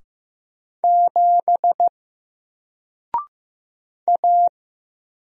Morse code, heard as a keyed tone, about 15 wpm, 700 Hz. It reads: 7 A
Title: Single Character (Letter:Number:Punctuation:Prosign) 15wpm